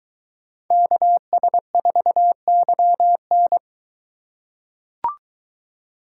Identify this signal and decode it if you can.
Morse code, heard as a keyed tone, about 22 wpm, 700 Hz. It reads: KS4YN